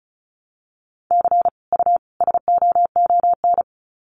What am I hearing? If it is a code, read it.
Morse code, heard as a keyed tone, about 34 wpm, 700 Hz. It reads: C U SOON
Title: Common QSO Phrases 35wpm